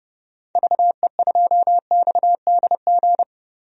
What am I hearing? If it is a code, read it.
Morse code, heard as a keyed tone, about 32 wpm, 700 Hz. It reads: VE2XDG
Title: DXCC-100 Call Signs - 6 Characters - Repeated 3x - 30wpm